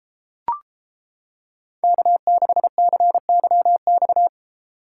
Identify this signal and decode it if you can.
Morse code, heard as a keyed tone, about 30 wpm, 700 Hz. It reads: K6CYX